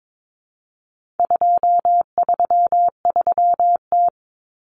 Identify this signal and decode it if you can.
Morse code, heard as a keyed tone, about 22 wpm, 700 Hz. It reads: 233T
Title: Common Cut Numbers - 4-Digit 22wpm